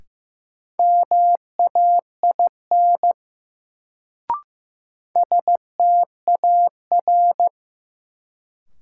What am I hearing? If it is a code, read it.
Morse code, heard as a keyed tone, about 15 wpm, 700 Hz. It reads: MAIN STAR
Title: Sentences from Top 1000 Words - Review Part 2 - New Word Drill - 15wpm